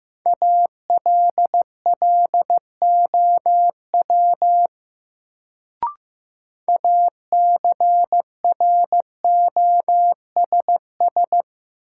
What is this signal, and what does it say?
Morse code, heard as a keyed tone, about 15 wpm, 700 Hz. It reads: ALLOW ACROSS
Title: Sentences from Top 700 Words - New Word Drill - 15wpm